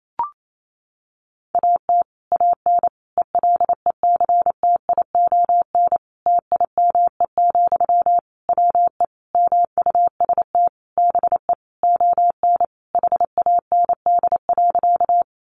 Morse code, heard as a keyed tone, about 28 wpm, 700 Hz. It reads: AT AN ELECTION TIME, WE MUST BE ON HAND.
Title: Sentences from Top 1500 Words 28wpm